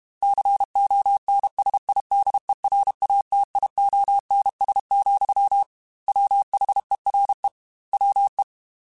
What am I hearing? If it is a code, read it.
Morse code, heard as a keyed tone, about 32 wpm, 800 Hz. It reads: CONSIDERATIONS, WHERE WE
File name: Technical Subjects / Technical Subjects-mp3s / Sw Pwr Sup @30